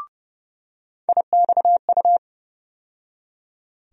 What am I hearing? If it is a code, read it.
Morse code, heard as a keyed tone, about 30 wpm, 700 Hz. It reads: IXU
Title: USA Call Sign Postfixes - 3 Letters 30wpm